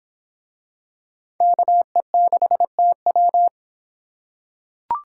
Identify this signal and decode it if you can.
Morse code, heard as a keyed tone, about 25 wpm, 700 Hz. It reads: KE6TW